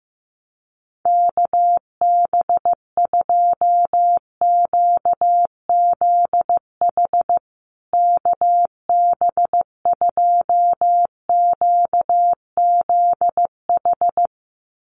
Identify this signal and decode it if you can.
Morse code, heard as a keyed tone, about 15 wpm, 700 Hz. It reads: KB2QZH KB2QZH
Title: USA Call Signs Repeated 2x 15wpm